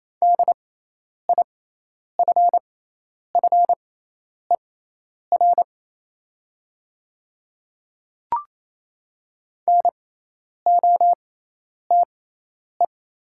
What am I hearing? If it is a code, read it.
Morse code, heard as a keyed tone, about 27 wpm, spaced as for 10 wpm, 690 Hz. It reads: DIFFER NOTE
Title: 6x Character Spacing - Top 500 Words - 28wpm